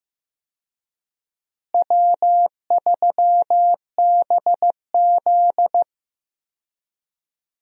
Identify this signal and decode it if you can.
Morse code, heard as a keyed tone, about 15 wpm, 700 Hz. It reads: W3BZ
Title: USA Call Signs - 4 Characters - Repeated 3x - 15wpm